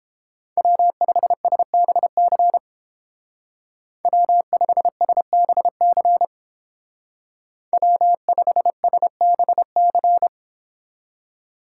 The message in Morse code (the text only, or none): W5SBC W5SBC W5SBC